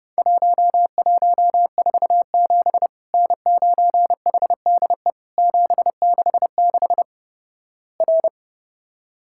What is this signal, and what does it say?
Morse code, heard as a keyed tone, about 30 wpm, 700 Hz. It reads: 1147 N9HDE 766 R
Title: Worked All Europe DX Contest - QTC Exchange - 30wpm